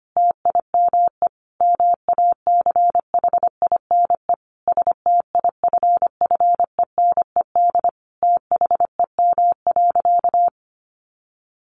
Morse code, heard as a keyed tone, about 25 wpm, 700 Hz. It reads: TIME MACHINE STIFFENED THEM.
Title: A Sound Of Thunder 25wpm